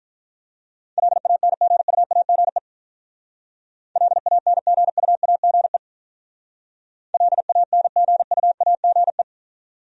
Morse code, heard as a keyed tone, about 44 wpm, 700 Hz. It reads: LANGUAGE LANGUAGE LANGUAGE
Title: Speed-Racing Top 500 Words 35wpm